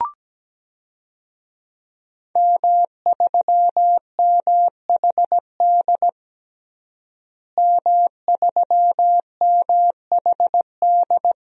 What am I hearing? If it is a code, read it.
Morse code, heard as a keyed tone, about 17 wpm, 700 Hz. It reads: M3MHD M3MHD
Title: DXCC-100 Call Signs - 5 Characters - Repeated 3x - 17wpm